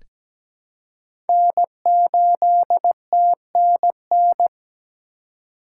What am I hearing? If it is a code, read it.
Morse code, heard as a keyed tone, about 17 wpm, 700 Hz. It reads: N8TNN
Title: USA Call Signs - 2x Character Spacing - 17wpm